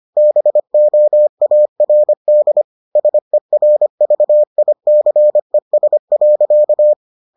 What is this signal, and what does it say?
Morse code, heard as a keyed tone, about 25 wpm, 600 Hz. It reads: BOARD SERVICES.